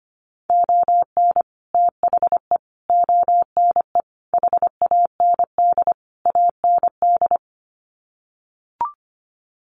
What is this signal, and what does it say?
Morse code, heard as a keyed tone, about 25 wpm, 700 Hz. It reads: ON THE ONE HAND AND
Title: Sets of 5 Words Encore2 25wpm